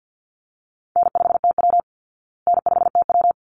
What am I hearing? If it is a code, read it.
Morse code, heard as a keyed tone, about 48 wpm, 700 Hz. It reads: N5TW N5TW
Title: Easy Call Signs Repeated 2x - 50wpm